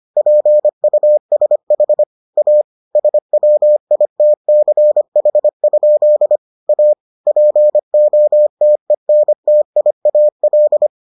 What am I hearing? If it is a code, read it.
Morse code, heard as a keyed tone, about 25 wpm, 600 Hz. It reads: PUSH A SWITCH? A POTENTIAL